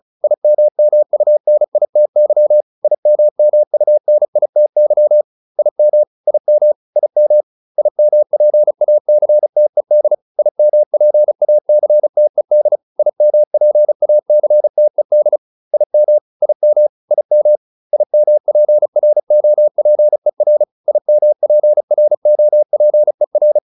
Morse code, heard as a keyed tone, about 35 wpm, 600 Hz. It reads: IMMUNITY IMMUNITY IM IM IM IMPACTED IMPACTED IMPACTED IM IM IM IMPROPER IMPROPER